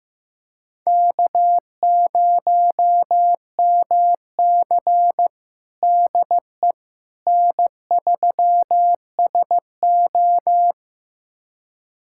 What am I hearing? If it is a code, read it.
Morse code, heard as a keyed tone, about 15 wpm, 700 Hz. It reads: K0MC DE N3SO
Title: USA Callsign DE Callsign - 4 Characters - 15wpm